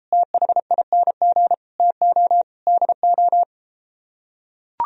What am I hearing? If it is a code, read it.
Morse code, heard as a keyed tone, about 33 wpm, 700 Hz. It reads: THING TO DO